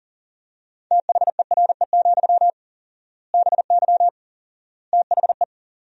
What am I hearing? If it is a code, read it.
Morse code, heard as a keyed tone, about 40 wpm, 700 Hz. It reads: THERE, BY THE
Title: 4x Word-Spacing - Sentences from Top 500 Words - 40wpm